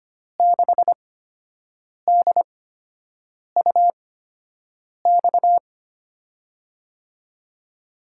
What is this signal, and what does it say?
Morse code, heard as a keyed tone, about 25 wpm, spaced as for 7 wpm, 700 Hz. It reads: 6DUX
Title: USA Call Signs - 6 Characters - 8x Character Spacing - 25wpm